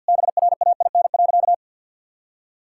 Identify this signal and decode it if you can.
Morse code, heard as a keyed tone, about 52 wpm, 700 Hz. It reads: BRAIN.
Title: Sentences from Top 1000 Words - Review Part 6 - 50wpm